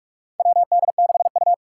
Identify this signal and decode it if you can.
Morse code, heard as a keyed tone, about 46 wpm, 700 Hz. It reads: WD6U
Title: USA Call Signs - 4 Characters - 6x Character Spacing - 45wpm